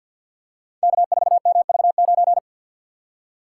Morse code, heard as a keyed tone, about 49 wpm, 700 Hz. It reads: K4MV9
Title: Code Groups - 5 Letter:Numbers - 50wpm